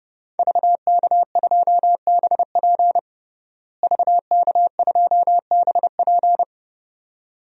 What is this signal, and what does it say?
Morse code, heard as a keyed tone, about 32 wpm, 700 Hz. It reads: VK2BP VK2BP
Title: DXCC-100 Call Signs - 5 Characters - Repeated 2x - 30wpm